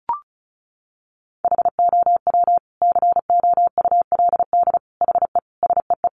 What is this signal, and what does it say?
Morse code, heard as a keyed tone, about 34 wpm, 700 Hz. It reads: HOW COULD HE SEE
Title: Sentences from Top 1300 Words 35wpm